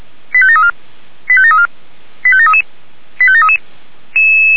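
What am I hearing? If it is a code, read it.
Five-tone selective call (ZVEI), heard as quick runs of five tones, tones of about 70 ms. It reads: 85734 85734 85733 85733
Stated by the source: ZVEI five-tone selective call received by SDR.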